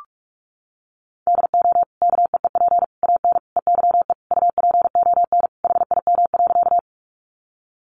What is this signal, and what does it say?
Morse code, heard as a keyed tone, about 44 wpm, 700 Hz. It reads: DO KEEP AN EYE UPON HIM.
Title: Sentences from Top 1100 Words 45wpm